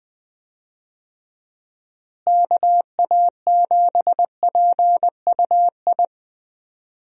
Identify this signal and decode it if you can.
Morse code, heard as a keyed tone, about 20 wpm, 700 Hz. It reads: KA7PUI
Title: USA Call Signs - 6 Characters - Repeated 3x - 20wpm